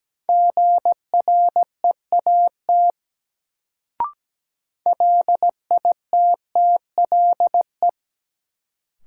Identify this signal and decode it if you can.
Morse code, heard as a keyed tone, about 17 wpm, 700 Hz. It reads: GREAT LITTLE
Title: Sets of 2 Words - New Word Drill - 17wpm